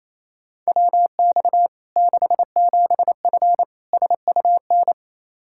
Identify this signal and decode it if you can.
Morse code, heard as a keyed tone, about 27 wpm, 700 Hz. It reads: WX 67F SUN
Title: K1USN SST QSO Practice 28wpm